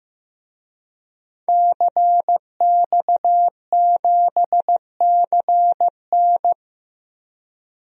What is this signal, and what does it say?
Morse code, heard as a keyed tone, about 15 wpm, 700 Hz. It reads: CX7CN
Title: DXCC-100 Call Signs - 5 Characters - Repeated 2x - 15wpm